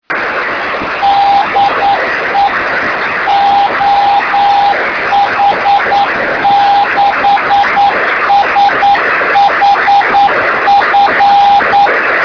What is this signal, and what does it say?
Morse code, heard as a keyed tone, about 9 wpm, 800 Hz, with noise. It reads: DE OH6SHF
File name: sm7lcb / 1744z oh6shf-b 23cm